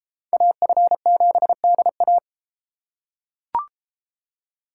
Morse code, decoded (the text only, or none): AF7DA